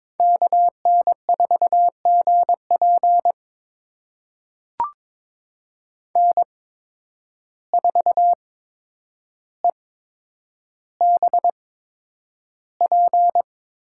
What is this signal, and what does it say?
Morse code, heard as a keyed tone, about 21 wpm, spaced as for 9 wpm, 700 Hz. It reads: KN4GP N4EBP
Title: USA Call Signs - 5 Characters - 8x Character Spacing - 22wpm